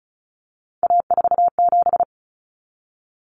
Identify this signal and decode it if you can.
Morse code, heard as a keyed tone, about 34 wpm, 700 Hz. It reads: A47
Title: Common Cut Numbers - 3-Digit 35wpm